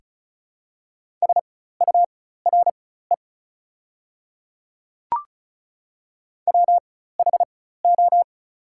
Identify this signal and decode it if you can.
Morse code, heard as a keyed tone, about 34 wpm, spaced as for 16 wpm, 690 Hz. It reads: SURE WHO
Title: 4x Character Spacing - Top 500 Words - 35wpm